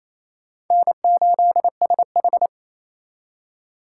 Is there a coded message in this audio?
Yes: Morse code from 0.7 to 2.5 s, heard as a keyed tone, about 36 wpm, 700 Hz. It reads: N8SH